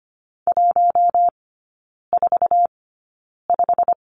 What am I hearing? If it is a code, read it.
Morse code, heard as a keyed tone, about 25 wpm, spaced as for 9 wpm, 700 Hz. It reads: 145
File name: Lesson 15e - Number 5 - Three Characters 25wpm